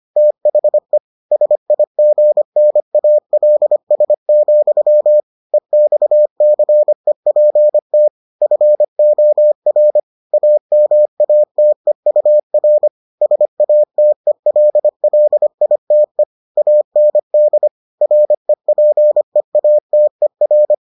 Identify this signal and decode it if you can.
Morse code, heard as a keyed tone, about 25 wpm, 600 Hz. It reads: THE SIGNALS, EXCEPT FOR AMATEUR SATELLITE AND REPEATER